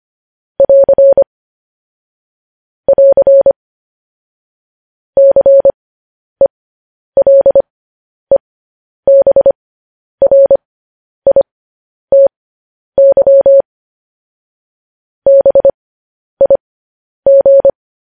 Morse code, heard as a keyed tone, about 25 wpm, spaced as for 10 wpm, 550 Hz. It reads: <AR> <AR> CELEBRITY BIG
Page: news headlines in morse code at 10 wpm